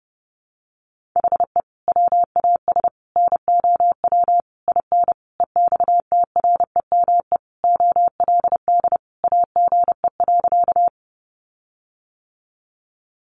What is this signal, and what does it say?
Morse code, heard as a keyed tone, about 30 wpm, 700 Hz. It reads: HE WAS NOW IN EXTREME OLD AGE.
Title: Sentences from Top 1300 Words 30wpm